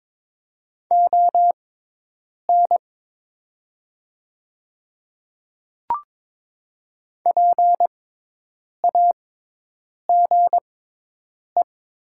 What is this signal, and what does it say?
Morse code, heard as a keyed tone, about 22 wpm, spaced as for 8 wpm, 700 Hz. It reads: ON PAGE
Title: 6x Character Spacing - Top 300 Words - 22wpm